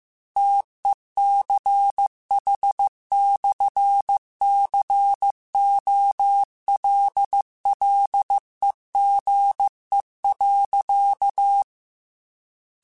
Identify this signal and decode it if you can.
Morse code, heard as a keyed tone, about 15 wpm, 800 Hz. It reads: TECH/COLLEGE.
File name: QSOs / QSOs-mp3s / WD0BGP 1@10